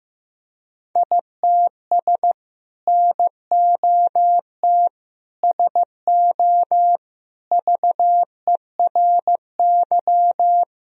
Morse code, heard as a keyed tone, about 15 wpm, 700 Hz. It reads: ITS NOT SO VERY
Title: Sentences from Top 1000 Words - Review Part 3 - 15wpm